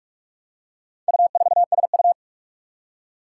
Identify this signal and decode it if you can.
Morse code, heard as a keyed tone, about 43 wpm, 700 Hz. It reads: U4SU
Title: Four Challenging Letter-Number Combinations 45wpm